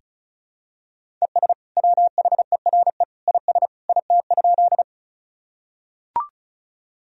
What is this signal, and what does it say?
Morse code, heard as a keyed tone, about 36 wpm, 700 Hz. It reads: ES WHERE IS IT?
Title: Sentences from Top 200 Words and Abbreviations 35wpm